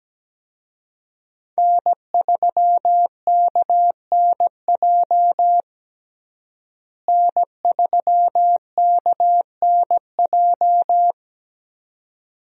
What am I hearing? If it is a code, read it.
Morse code, heard as a keyed tone, about 17 wpm, 700 Hz. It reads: N3KNJ N3KNJ
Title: USA Call Signs - 5 Characters - Repeated 2x - 17wpm